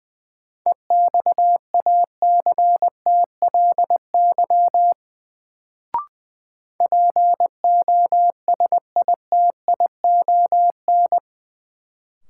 Morse code, heard as a keyed tone, about 20 wpm, 700 Hz. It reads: EXACTLY POSITION
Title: Sentences from Top 900 Words - New Word Drill - 20wpm